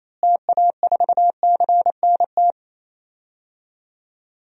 Morse code, heard as a keyed tone, about 28 wpm, 700 Hz. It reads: TA4CNT
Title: DXCC-100 Call Signs - 6 Characters - Repeated 2x - 28wpm